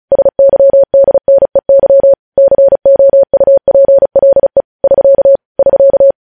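Morse code, heard as a keyed tone, about 36 wpm, 550 Hz. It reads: SYDNEY COUPLE <SK> <SK>
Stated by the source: Morse code practice reading news headlines at 35 words per minute.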